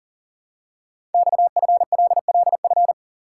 Morse code, heard as a keyed tone, about 38 wpm, 700 Hz. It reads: XFLLF
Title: Code Groups - 5 Letters - 40wpm